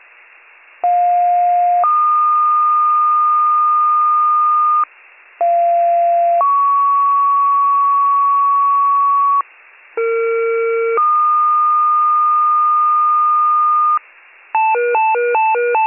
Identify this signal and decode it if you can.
Two-tone page, heard as a one-second tone, then a three-second tone, about 710 then 1220 Hz, about 690 then 1120 Hz, about 460 then 1220 Hz. It is heard from 0.8 to 14.0 s.